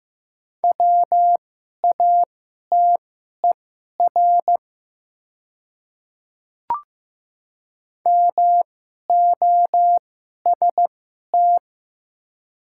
Morse code, heard as a keyed tone, about 15 wpm, spaced as for 11 wpm, 700 Hz. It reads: WATER MOST
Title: 2x Character Spacing - Top 100 Words - 15wpm